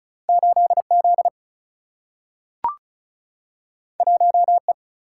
Morse code, heard as a keyed tone, about 36 wpm, 700 Hz. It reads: 8 Z 1 E
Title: Single-Letter Single-Number 35wpm